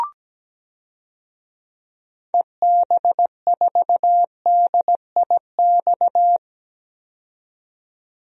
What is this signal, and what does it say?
Morse code, heard as a keyed tone, about 17 wpm, 700 Hz. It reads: EB4DIX
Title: DXCC-100 Call Signs - 6 Characters - 17wpm